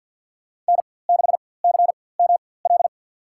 Morse code, heard as a keyed tone, about 53 wpm, spaced as for 24 wpm, 700 Hz. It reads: N6CML
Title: USA Call Signs - 4x Character Spacing - 50wpm